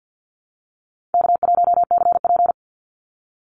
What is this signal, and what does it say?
Morse code, heard as a keyed tone, about 48 wpm, 700 Hz. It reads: K1CP
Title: Easy Call Signs Repeated 2x - 50wpm